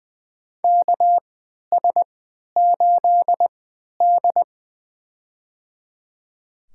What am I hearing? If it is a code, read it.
Morse code, heard as a keyed tone, about 20 wpm, spaced as for 11 wpm, 700 Hz. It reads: KS8D
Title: USA Call Signs - 4 Characters - 3x Character Spacing - 20wpm